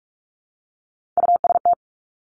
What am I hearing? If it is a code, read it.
Morse code, heard as a keyed tone, about 46 wpm, 700 Hz. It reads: UST
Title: Top Three Letter Combinations in Words 45wpm